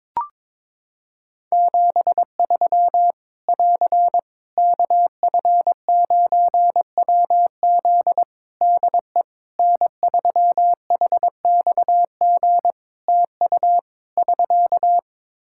Morse code, heard as a keyed tone, about 22 wpm, 700 Hz. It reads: 73 <AR> KF9WZ DE N3HXG TU <SK>
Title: Practice QSO - Exchange 4 - Ending - 22wpm